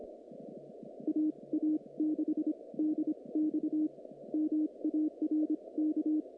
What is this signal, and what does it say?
Morse code, heard as a keyed tone, about 25 wpm, spaced as for 21 wpm, 310 Hz. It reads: AA6DX MARK
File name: contests / naqp-cw-k5tr-op-k5pi- -2200